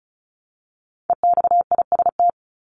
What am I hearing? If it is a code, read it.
Morse code, heard as a keyed tone, about 34 wpm, 700 Hz. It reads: EXIST
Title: ING Suffix - Top 100 Words with Base Words 35wpm